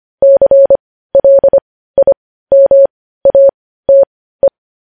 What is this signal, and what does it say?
Morse code, heard as a keyed tone, about 25 wpm, spaced as for 15 wpm, 550 Hz. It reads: CLIMATE
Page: news headlines in morse code at 15 wpm